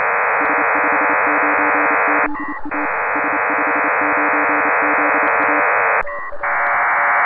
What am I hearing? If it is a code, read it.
Morse code, heard as a keyed tone, about 28 wpm, 300 Hz, with noise. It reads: S59ZA S59ZA
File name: Pile-up with PACTOR3